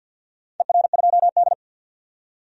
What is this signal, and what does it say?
Morse code, heard as a keyed tone, about 53 wpm, 700 Hz. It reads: ER2D
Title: DXCC-100 Call Signs - 4 Characters - 50wpm